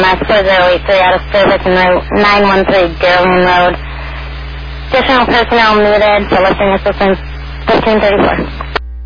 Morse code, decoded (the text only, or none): none